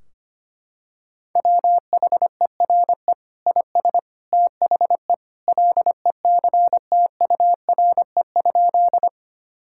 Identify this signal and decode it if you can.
Morse code, heard as a keyed tone, about 25 wpm, 700 Hz. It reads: WHERE IS THE LECTURE?
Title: Sentences from Top 1600 Words 25wpm